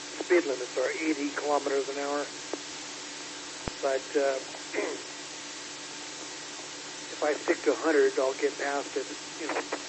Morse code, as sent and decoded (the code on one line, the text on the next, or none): none